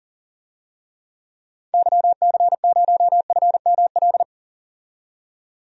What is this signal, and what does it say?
Morse code, heard as a keyed tone, about 40 wpm, 700 Hz. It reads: YC0FML
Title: DXCC-100 Call Signs - 6 Characters - 40wpm